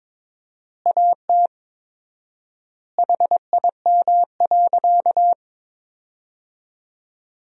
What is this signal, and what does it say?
Morse code, heard as a keyed tone, about 22 wpm, 700 Hz. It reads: AT HIM.
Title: 4x Word-Spacing - Sentences from Top 100 Words - 22wpm